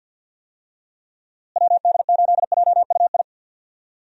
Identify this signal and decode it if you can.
Morse code, heard as a keyed tone, about 51 wpm, 700 Hz. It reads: WD8JUI
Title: USA Call Signs - 6 Characters - Repeated 3x - 50wpm